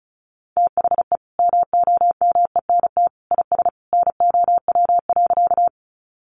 Morse code, heard as a keyed tone, about 34 wpm, 700 Hz. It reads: THE MOMENT IS NOW.
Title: Sentences from Top 1000 Words 35wpm